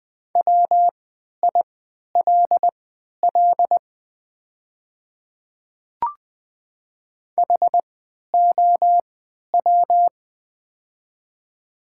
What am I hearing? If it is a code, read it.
Morse code, heard as a keyed tone, about 20 wpm, spaced as for 11 wpm, 700 Hz. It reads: WILL HOW
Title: 3x Character Spacing - Top 100 Words - 20wpm